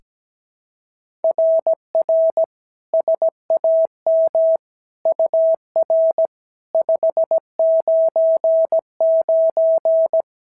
Morse code, heard as a keyed tone, about 17 wpm, 650 Hz. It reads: RR SAM UR 599
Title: SOTA QSO Practice 17wpm